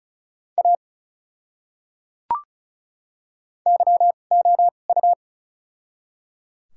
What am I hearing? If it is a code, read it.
Morse code, heard as a keyed tone, about 36 wpm, spaced as for 25 wpm, 700 Hz. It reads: A YOU
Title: 2x Character Spacing - Top 100 Words - 35wpm